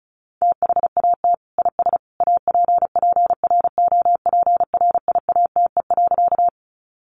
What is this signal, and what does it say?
Morse code, heard as a keyed tone, about 34 wpm, 700 Hz. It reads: THAT IS APPROPRIATE.
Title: Sentences from Top 900 Words 35wpm